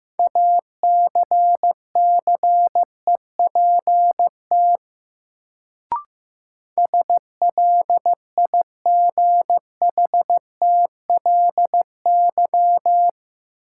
Morse code, heard as a keyed tone, about 15 wpm, 700 Hz. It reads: ACCEPT SLIGHTLY
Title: Sentences from Top 1300 Words - New Word Drill Alt - 15wpm